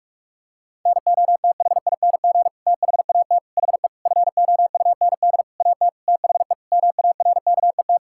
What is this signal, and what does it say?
Morse code, heard as a keyed tone, about 46 wpm, 700 Hz. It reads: NOTHING THAT HE FOUND AT THE MARKET
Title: Sentences from Top 1000 Words - Review Part 6 - 45wpm